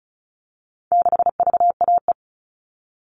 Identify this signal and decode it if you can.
Morse code, heard as a keyed tone, about 34 wpm, 700 Hz. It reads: 6VAE
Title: All Cut Numbers - 4-Digit 35wpm